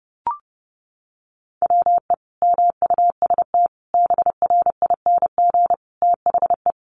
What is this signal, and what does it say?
Morse code, heard as a keyed tone, about 30 wpm, 700 Hz. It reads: WE MUST BRING THE
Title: Sentences from Top 1200 Words 30wpm